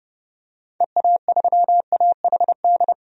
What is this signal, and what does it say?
Morse code, heard as a keyed tone, about 30 wpm, 700 Hz. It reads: EA3AHD
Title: DXCC-100 Call Signs - 6 Characters - 30wpm